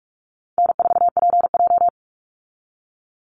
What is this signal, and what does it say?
Morse code, heard as a keyed tone, about 44 wpm, 700 Hz. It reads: N4PJ
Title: Easy Call Signs Repeated 3x - 45wpm